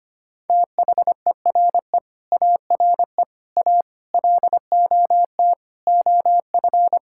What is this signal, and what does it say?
Morse code, heard as a keyed tone, about 25 wpm, 700 Hz. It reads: THERE ARE A LOT OF
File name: Sets of 9 Words - 25wpm